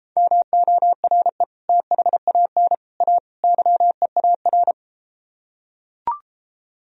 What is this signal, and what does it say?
Morse code, heard as a keyed tone, about 34 wpm, 700 Hz. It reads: MORE THAN A YEAR